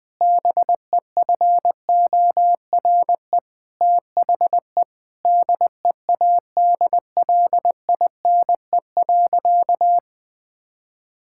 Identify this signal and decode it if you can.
Morse code, heard as a keyed tone, about 20 wpm, 700 Hz. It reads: BEFORE THE DEADLINE.
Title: Sentences with 8-Word Phrases - Spoken Beginning - 20wpm